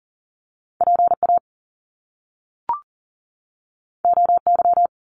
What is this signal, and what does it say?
Morse code, heard as a keyed tone, about 40 wpm, 700 Hz. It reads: PA OY
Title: Top Two Letter Combinations in Words 40wpm